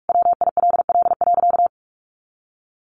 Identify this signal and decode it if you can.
Morse code, heard as a keyed tone, about 44 wpm, 700 Hz. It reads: WILL.
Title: Sentences from Top 1500 Words 45wpm